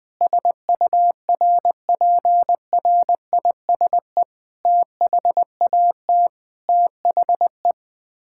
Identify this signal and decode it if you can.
Morse code, heard as a keyed tone, about 20 wpm, 700 Hz. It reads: SURPRISE THAT THE
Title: Sets of 9 Words - 20wpm